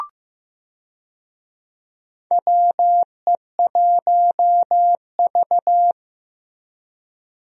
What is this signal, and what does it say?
Morse code, heard as a keyed tone, about 15 wpm, 700 Hz. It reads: WE1V
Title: USA Call Signs - 4 Characters - Repeated 3x - 15wpm